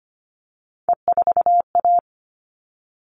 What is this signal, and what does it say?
Morse code, heard as a keyed tone, about 25 wpm, 700 Hz. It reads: E4A
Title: All Cut Numbers - 3-Digit 25wpm